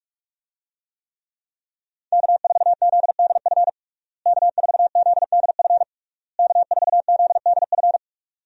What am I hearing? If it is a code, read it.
Morse code, heard as a keyed tone, about 46 wpm, 700 Hz. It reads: K4ZDF K4ZDF K4ZDF
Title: USA Call Signs - 5 Characters - Repeated 3x - 45wpm